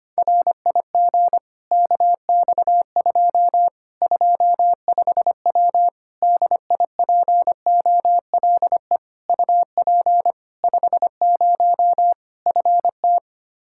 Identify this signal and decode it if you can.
Morse code, heard as a keyed tone, about 25 wpm, 700 Hz. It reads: RIG KX2 25W DIPOLE UP 50 FT
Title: K1USN SST QSO Practice 25wpm